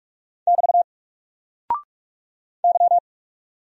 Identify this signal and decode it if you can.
Morse code, heard as a keyed tone, about 44 wpm, 700 Hz. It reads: <BT> Y
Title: Single Character (Letter:Number:Punctuation:Prosign) 45wpm